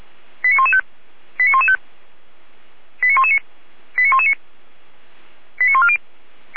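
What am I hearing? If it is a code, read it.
Five-tone selective call (ZVEI), heard as quick runs of five tones, tones of about 80 ms. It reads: 89115 89115 89119 89119 89140
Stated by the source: ZVEI five-tone call received by SDR.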